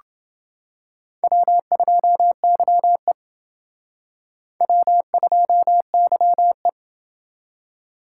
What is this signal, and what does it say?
Morse code, heard as a keyed tone, about 29 wpm, 700 Hz. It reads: W2YE W2YE